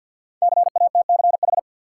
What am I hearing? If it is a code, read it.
Morse code, heard as a keyed tone, about 52 wpm, 700 Hz. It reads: CATCH
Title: Sentences from Top 1500 Words - New Word Drill - 50wpm